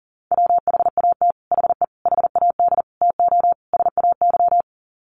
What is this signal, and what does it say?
Morse code, heard as a keyed tone, about 40 wpm, 700 Hz. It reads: WHAT HE HAD TO SAY
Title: Sets of 5 Words Encore2 40wpm